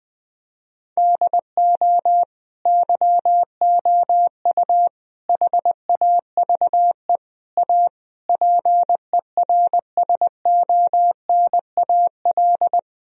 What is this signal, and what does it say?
Morse code, heard as a keyed tone, about 20 wpm, 700 Hz. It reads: DO YOU HAVE A PERSONAL
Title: Sentences from Top 1000 Words - Review Part 4 - 20wpm